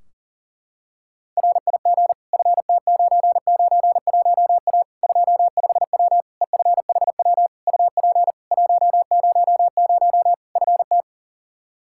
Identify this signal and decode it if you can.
Morse code, heard as a keyed tone, about 40 wpm, 700 Hz. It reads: RIG FT991A 25W EFHW UP 100 FT
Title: Practice QSO - Exchange 2 - Info - 40wpm